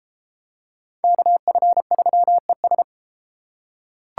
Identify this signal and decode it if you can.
Morse code, heard as a keyed tone, about 29 wpm, 700 Hz. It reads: KF3ES